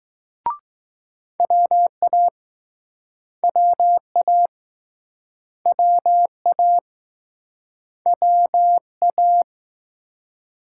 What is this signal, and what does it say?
Morse code, heard as a keyed tone, about 20 wpm, 700 Hz. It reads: WA WA WA WA